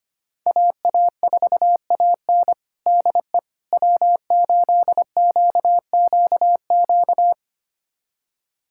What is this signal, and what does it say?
Morse code, heard as a keyed tone, about 25 wpm, 700 Hz. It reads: AA4AN DE W8QQQ
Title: USA Callsign DE Callsign 25wpm